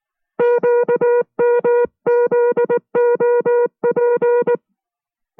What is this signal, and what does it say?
Morse code, heard as a keyed tone, about 19 wpm, 480 Hz. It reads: QMZOP